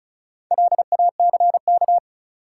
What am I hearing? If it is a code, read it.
Morse code, heard as a keyed tone, about 36 wpm, 700 Hz. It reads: LACK
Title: Sentences from Top 1100 Words - New Word Drill Alt - 35wpm